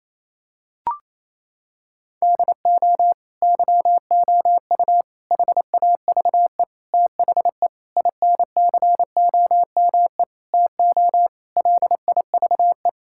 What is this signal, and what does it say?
Morse code, heard as a keyed tone, about 28 wpm, 700 Hz. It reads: DO YOU HAVE THE INCOME TO LIVE
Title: Sentences from Top 1000 Words - Review Part 5 - 28wpm